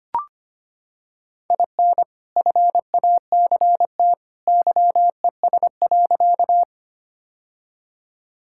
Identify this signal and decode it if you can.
Morse code, heard as a keyed tone, about 25 wpm, 690 Hz. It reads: IN FACT YES.
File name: practice / 25wpm / Sentences with 2-Word Phrases - 25wpm